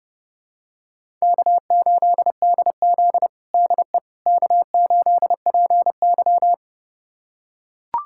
Morse code, heard as a keyed tone, about 32 wpm, 700 Hz. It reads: K8DZ DE K8PY
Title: USA Callsign DE Callsign - 4 Characters - 30wpm